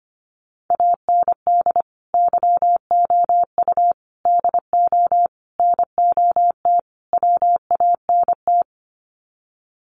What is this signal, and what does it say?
Morse code, heard as a keyed tone, about 25 wpm, 700 Hz. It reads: AND YOU DO NOT WANT
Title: Sets of 5 Words Encore2 25wpm